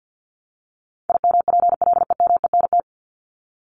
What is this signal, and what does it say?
Morse code, heard as a keyed tone, about 50 wpm, 700 Hz. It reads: IMPLEMENT
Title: Top 1501-1750 Words 50wpm